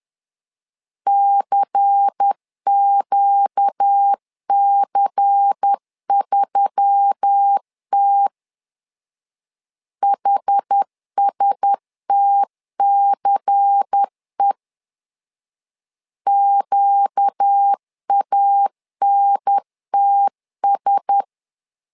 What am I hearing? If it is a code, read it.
Morse code, heard as a keyed tone, about 10 wpm, 800 Hz. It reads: CQC3T HSTCE QANTS
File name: LEZIONI / 5 LBVC / 5 LBVC 10wpm 2349